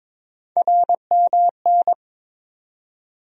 Morse code, heard as a keyed tone, about 22 wpm, 700 Hz. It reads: RMN
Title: USA Call Sign Postfixes - 3 Letters 22wpm